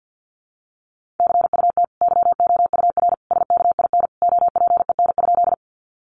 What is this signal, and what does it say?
Morse code, heard as a keyed tone, about 48 wpm, 700 Hz. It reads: CUT YOUR SKIN OPEN?
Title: Sentences from Top 1300 Words 50wpm